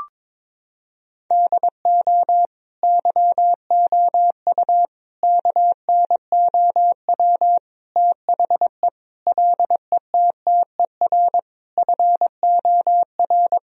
Morse code, heard as a keyed tone, about 22 wpm, 700 Hz. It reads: DO YOU KNOW THE LETTER FOR
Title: Sentences from Top 1000 Words - Review Part 2 - 22wpm